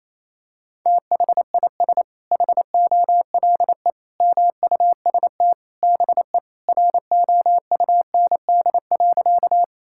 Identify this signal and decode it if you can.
Morse code, heard as a keyed tone, about 28 wpm, 700 Hz. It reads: THIS HOLE MUST BE ROUND.
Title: Sentences from Top 1600 Words 28wpm